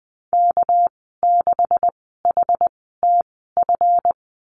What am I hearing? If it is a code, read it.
Morse code, heard as a keyed tone, about 19 wpm, spaced as for 14 wpm, 700 Hz. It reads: K6HTF
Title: 207 Lesson 33g - Letter j - Call Signs 20wpm